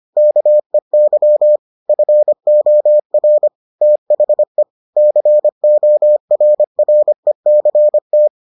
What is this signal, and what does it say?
Morse code, heard as a keyed tone, about 25 wpm, 600 Hz. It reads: KEY FOR THE CORRECT